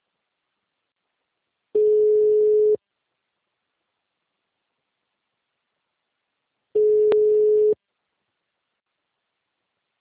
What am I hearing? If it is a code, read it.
Ringback tone (Europe, 425 Hz).
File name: out- -2012- - - .26123